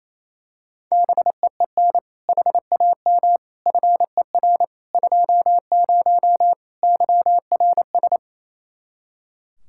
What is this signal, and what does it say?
Morse code, heard as a keyed tone, about 27 wpm, 700 Hz. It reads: BEEN HAM FER 20 YRS
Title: Practice QSO - Exchange 3 - More Info - 28wpm